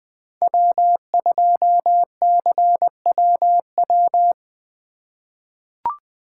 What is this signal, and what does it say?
Morse code, heard as a keyed tone, about 20 wpm, 700 Hz. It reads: W2CWW
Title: USA Call Signs - 5 Characters - 8x Character Spacing - 20wpm